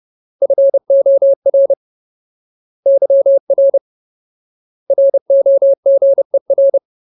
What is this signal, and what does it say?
Morse code, heard as a keyed tone, about 30 wpm, 550 Hz. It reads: FOR YR ROGER
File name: Ultimate Word List / 30WPM Ultimate